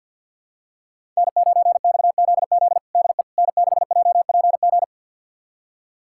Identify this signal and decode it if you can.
Morse code, heard as a keyed tone, about 47 wpm, 700 Hz. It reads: N9XZZ DE N6JPG
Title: USA Callsign DE Callsign - 5 Characters - 50wpm